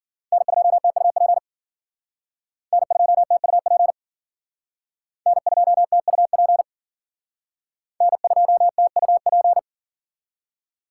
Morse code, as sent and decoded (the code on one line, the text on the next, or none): -. ..--- - ..- .--. / -. ..--- - ..- .--. / -. ..--- - ..- .--. / -. ..--- - ..- .--.
N2TUP N2TUP N2TUP N2TUP